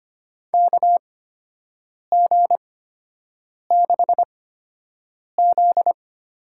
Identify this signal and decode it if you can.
Morse code, heard as a keyed tone, about 25 wpm, spaced as for 7 wpm, 700 Hz. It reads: KG6Z
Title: USA Call Signs - 6 Characters - 8x Character Spacing - 25wpm